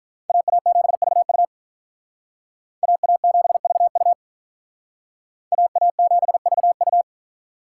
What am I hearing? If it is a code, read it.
Morse code, heard as a keyed tone, about 46 wpm, 700 Hz. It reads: AA7VU AA7VU AA7VU